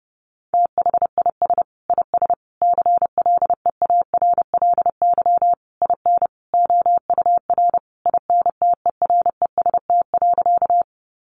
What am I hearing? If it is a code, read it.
Morse code, heard as a keyed tone, about 30 wpm, 700 Hz. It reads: THIS IS CLEARLY IN OUR INTEREST.
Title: Sentences from Top 1300 Words 30wpm